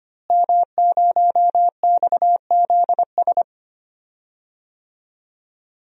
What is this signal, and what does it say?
Morse code, heard as a keyed tone, about 25 wpm, 700 Hz. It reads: M0XZS
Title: DXCC-100 Call Signs - 5 Characters - Repeated 2x - 25wpm